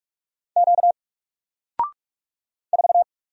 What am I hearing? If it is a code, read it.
Morse code, heard as a keyed tone, about 45 wpm, 700 Hz. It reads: Q 4
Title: Single Letter-Number Alt 45wpm